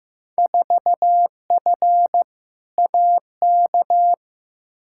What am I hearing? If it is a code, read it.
Morse code, heard as a keyed tone, about 15 wpm, 700 Hz. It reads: 4F AK
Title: Field Day Exchanges Encore - 15wpm